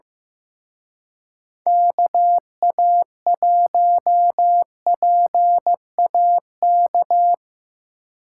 Morse code, heard as a keyed tone, about 17 wpm, 700 Hz. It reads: KA1PAK
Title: Speed Racing - USA 2x3 Call Signs 15wpm